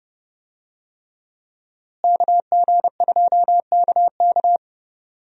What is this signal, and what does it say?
Morse code, heard as a keyed tone, about 30 wpm, 700 Hz. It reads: KG2KK
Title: USA Call Signs - 5 Characters - Repeated 3x - 30wpm